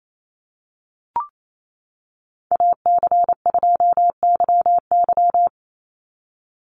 Morse code, heard as a keyed tone, about 28 wpm, 700 Hz. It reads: AC2YY
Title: USA Call Signs 28wpm